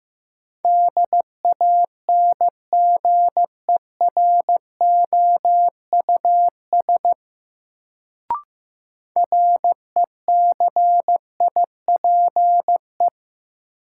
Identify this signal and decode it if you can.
Morse code, heard as a keyed tone, about 15 wpm, 700 Hz. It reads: DANGEROUS RECIPE
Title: Sentences from Top 1200 Words - New Word Drill Alt - 15wpm